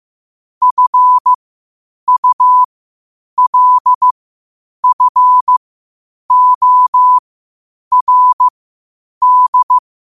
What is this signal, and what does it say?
Morse code, heard as a keyed tone, about 15 wpm, spaced as for 8 wpm, 1000 Hz. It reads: FULFORD